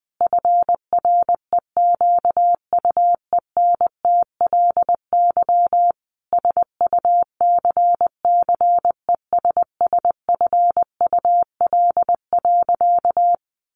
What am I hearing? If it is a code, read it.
Morse code, heard as a keyed tone, about 20 wpm, 700 Hz. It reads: FREQUENTLY SUCCESSFUL.
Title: Sentences from Top 1300 Words 20wpm